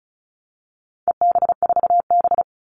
Morse code, heard as a keyed tone, about 34 wpm, 700 Hz. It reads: EB4B
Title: All Cut Numbers - 4-Digit 35wpm